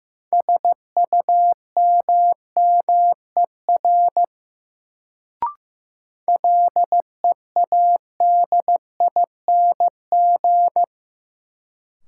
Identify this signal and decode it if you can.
Morse code, heard as a keyed tone, about 15 wpm, 700 Hz. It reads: SUMMER LEADING
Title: Sentences from Top 1100 Words - New Word Drill Alt - 15wpm